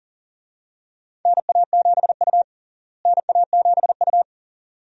Morse code, heard as a keyed tone, about 40 wpm, 700 Hz. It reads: NA7U NA7U
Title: USA Call Signs - 4 Characters - Repeated 2x - 40wpm